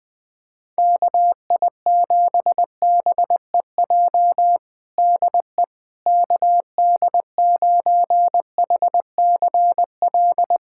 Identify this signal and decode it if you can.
Morse code, heard as a keyed tone, about 20 wpm, 700 Hz. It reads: KI7BEJ DE KD9HCL
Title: USA Callsign DE Callsign - 6 Characters - 20wpm